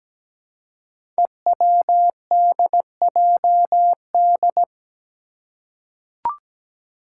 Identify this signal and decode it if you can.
Morse code, heard as a keyed tone, about 17 wpm, 700 Hz. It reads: EWDJD
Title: Five Challenging Letter-Number Combinations 17wpm